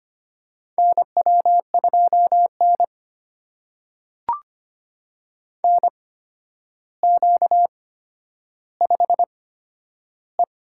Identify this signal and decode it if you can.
Morse code, heard as a keyed tone, about 25 wpm, spaced as for 11 wpm, 700 Hz. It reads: NW2N NQ5E
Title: USA Call Signs - 4 Characters - 8x Character Spacing - 25wpm